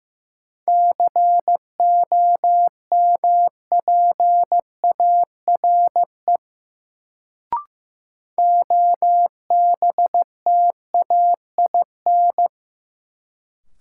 Morse code, heard as a keyed tone, about 15 wpm, 700 Hz. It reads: COMPARE OBTAIN
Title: Sentences from Top 1300 Words - New Word Drill Alt - 15wpm